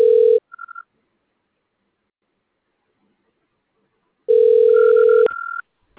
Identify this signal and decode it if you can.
Ringback tone (North America), 440 + 480 Hz.
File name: recordings / RE334d97d53d6fabe c243